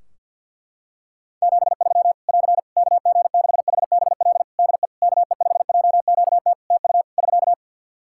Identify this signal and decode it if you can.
Morse code, heard as a keyed tone, about 47 wpm, 700 Hz. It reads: 73 <AR> KG6HBL DE KE5JQT TU <SK>
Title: Practice QSO - Exchange 4 - Ending - 50wpm